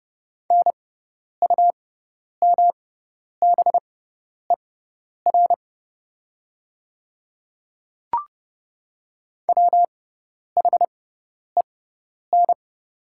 Morse code, heard as a keyed tone, about 32 wpm, spaced as for 10 wpm, 700 Hz. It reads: NUMBER WHEN
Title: 6x Character Spacing - Top 100 Words - 30wpm